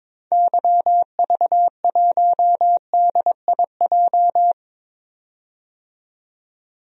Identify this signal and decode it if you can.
Morse code, heard as a keyed tone, about 22 wpm, 700 Hz. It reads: YV1DIJ
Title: DXCC-100 Call Signs - 6 Characters - 22wpm